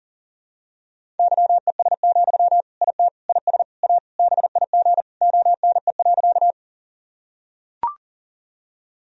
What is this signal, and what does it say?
Morse code, heard as a keyed tone, about 37 wpm, 700 Hz. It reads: YES, IT IS A BIG ONE.
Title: Sentences from Top 1000 Words - Review Part 1 - 40wpm